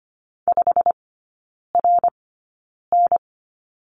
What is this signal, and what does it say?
Morse code, heard as a keyed tone, about 25 wpm, spaced as for 9 wpm, 690 Hz. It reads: 5RN
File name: Lesson 15e - Number 5 - Three Characters 25wpm